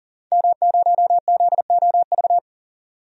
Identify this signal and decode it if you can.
Morse code, heard as a keyed tone, about 39 wpm, 700 Hz. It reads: M0ZOV
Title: DXCC-100 Call Signs - 5 Characters - 40wpm